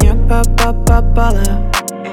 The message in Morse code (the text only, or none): none